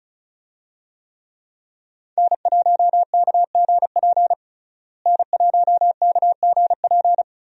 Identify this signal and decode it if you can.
Morse code, heard as a keyed tone, about 36 wpm, 700 Hz. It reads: N1KGP N1KGP
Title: USA Call Signs - 5 Characters - Repeated 3x - 35wpm